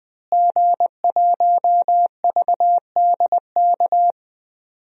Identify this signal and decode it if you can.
Morse code, heard as a keyed tone, about 20 wpm, 700 Hz. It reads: G1VDK
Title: DXCC-100 Call Signs - 5 Characters - Repeated 3x - 20wpm